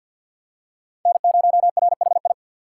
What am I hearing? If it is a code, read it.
Morse code, heard as a keyed tone, about 51 wpm, 700 Hz. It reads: N0RHI
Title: USA Call Signs - 3x Character Spacing - 50wpm